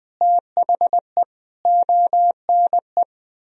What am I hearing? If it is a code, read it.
Morse code, heard as a keyed tone, about 20 wpm, 700 Hz. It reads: THE ONE
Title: Sentences from Top 1000 Words - Review Part 2 - 20wpm